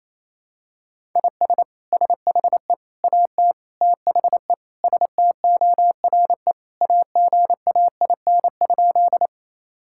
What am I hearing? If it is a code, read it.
Morse code, heard as a keyed tone, about 28 wpm, 700 Hz. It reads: IS SHE AT THE STORE AGAIN?
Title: Sentences from Top 1000 Words - Review Part 5 - 28wpm